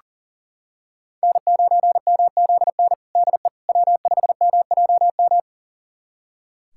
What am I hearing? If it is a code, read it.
Morse code, heard as a keyed tone, about 40 wpm, 700 Hz. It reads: N9MZN DE W5MJM
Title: USA Callsign DE Callsign - 5 Characters - 40wpm